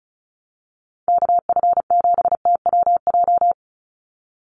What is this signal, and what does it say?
Morse code, heard as a keyed tone, about 34 wpm, 700 Hz. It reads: KF7TWJ
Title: USA Call Signs 35wpm